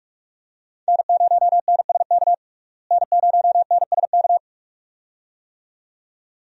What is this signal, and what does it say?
Morse code, heard as a keyed tone, about 42 wpm, 700 Hz. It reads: N0NSK N0NSK
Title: USA Call Signs - 5 Characters - Repeated 2x - 45wpm